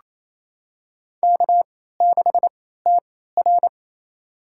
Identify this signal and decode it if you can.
Morse code, heard as a keyed tone, about 28 wpm, spaced as for 16 wpm, 700 Hz. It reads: K6TR
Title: USA Call Signs - 4 Characters - 3x Character Spacing - 28wpm